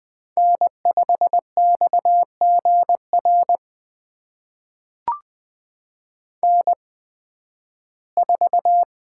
Morse code, heard as a keyed tone, about 20 wpm, spaced as for 13 wpm, 700 Hz. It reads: N5XGR N4
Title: USA Call Signs - 5 Characters - 8x Character Spacing - 20wpm